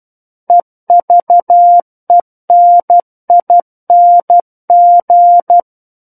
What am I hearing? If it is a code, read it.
Morse code, heard as a keyed tone, about 12 wpm, 700 Hz. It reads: EVENING
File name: Book One / 03-Chapter 3 - 12 WPM